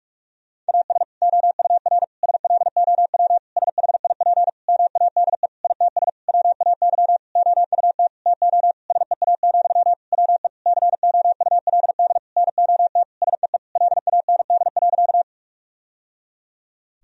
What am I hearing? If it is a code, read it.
Morse code, heard as a keyed tone, about 46 wpm, 700 Hz. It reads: AS OUR SLOW SHIP MADE ITS WAY OUT TO SEA, WE COULD NOT SEE LAND.
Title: Sentences from Top 1000 Words - Review Part 2 - 45wpm